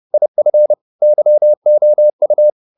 Morse code, heard as a keyed tone, about 30 wpm, 600 Hz. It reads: IF YOU